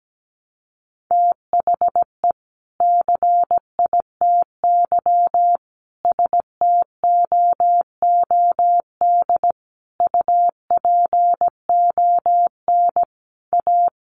Morse code, heard as a keyed tone, about 17 wpm, 700 Hz. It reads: THE CITY STOOD UPON A
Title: Sentences from Top 1100 Words 17wpm